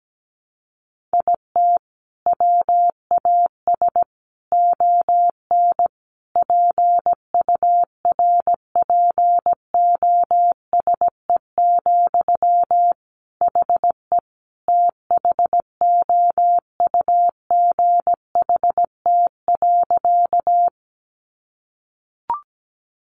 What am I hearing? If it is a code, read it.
Morse code, heard as a keyed tone, about 17 wpm, 700 Hz. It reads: IT WAS ON PURPOSE, HE THOUGHT.
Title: Sentences from Top 800 Words 17wpm